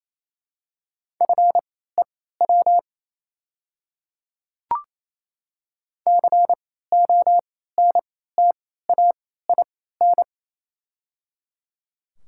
Morse code, heard as a keyed tone, about 27 wpm, spaced as for 16 wpm, 690 Hz. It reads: FEW CONTAIN
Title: 3x Character Spacing - Top 500 Words - 28wpm